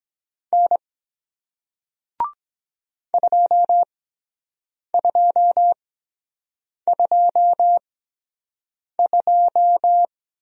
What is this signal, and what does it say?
Morse code, heard as a keyed tone, about 23 wpm, spaced as for 7 wpm, 700 Hz. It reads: N 2222